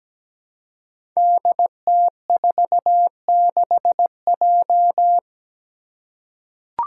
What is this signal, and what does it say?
Morse code, heard as a keyed tone, about 17 wpm, 700 Hz. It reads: DT46J
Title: Five Challenging Letter-Number Combinations 17wpm